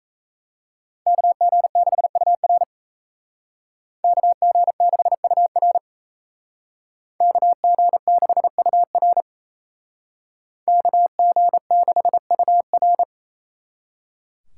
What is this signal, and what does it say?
Morse code, heard as a keyed tone, about 36 wpm, 700 Hz. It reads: KG6UR KG6UR KG6UR KG6UR